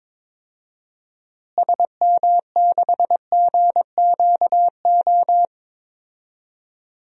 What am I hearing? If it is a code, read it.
Morse code, heard as a keyed tone, about 22 wpm, 700 Hz. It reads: SM6GQO
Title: DXCC-100 Call Signs - 6 Characters - Repeated 2x - 22wpm